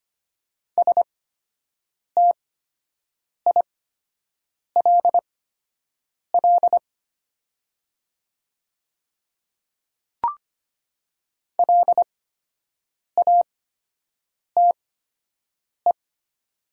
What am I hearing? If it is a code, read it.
Morse code, heard as a keyed tone, about 25 wpm, spaced as for 7 wpm, 700 Hz. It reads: STILL LATE
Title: 8x Character Spacing - Top 300 Words - 25wpm